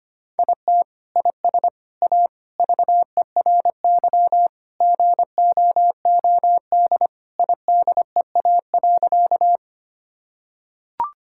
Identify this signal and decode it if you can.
Morse code, heard as a keyed tone, about 25 wpm, 700 Hz. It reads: IT IS A VERY GOOD IDEA.
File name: practice / 25wpm / Sentences with 3-Word Phrases - 25wpm